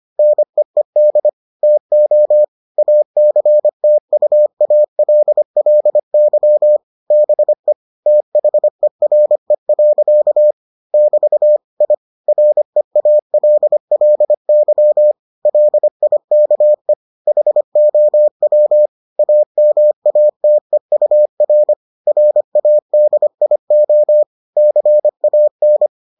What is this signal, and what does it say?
Morse code, heard as a keyed tone, about 25 wpm, 600 Hz. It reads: NEED TO ACTUALLY BE THERE. <BT> I REALLY LIKE HOW AMATEUR RADIO CAN